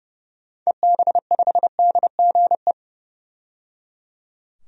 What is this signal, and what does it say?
Morse code, heard as a keyed tone, about 30 wpm, 700 Hz. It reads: EB5DGE
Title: DXCC-100 Call Signs - 6 Characters - 30wpm